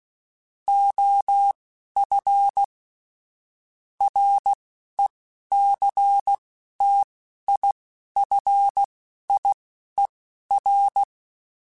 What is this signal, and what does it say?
Morse code, heard as a keyed tone, about 16 wpm, spaced as for 11 wpm, 800 Hz. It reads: OF RECTIFIER
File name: Technical Subjects / Technical Subjects-mp3s / Sw Pwr Sup @10